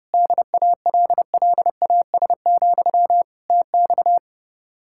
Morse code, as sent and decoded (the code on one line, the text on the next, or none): -.. .- .-.. .-.. .- ... --..-- / - -..-
DALLAS, TX